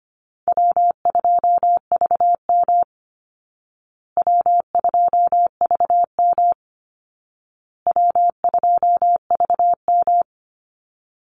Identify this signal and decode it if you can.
Morse code, heard as a keyed tone, about 25 wpm, 700 Hz. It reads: W2VM W2VM W2VM
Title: Call Signs Repeated 2x - 25wpm